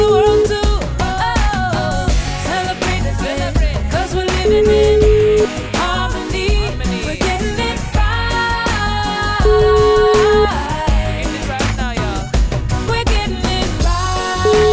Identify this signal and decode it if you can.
Ringback tone (Europe, 425 Hz), with noise.